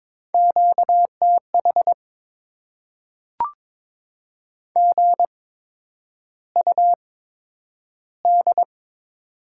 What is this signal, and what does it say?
Morse code, heard as a keyed tone, about 22 wpm, spaced as for 9 wpm, 700 Hz. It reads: QTH GUD
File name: Top 20 QSO Elements - 8x Character Spacing - 22wpm